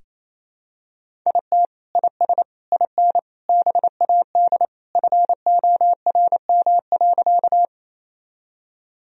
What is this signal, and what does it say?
Morse code, heard as a keyed tone, about 28 wpm, 700 Hz. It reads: IT IS IN BAD FORM.
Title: Sentences from Top 1000 Words - Review Part 1 - 28wpm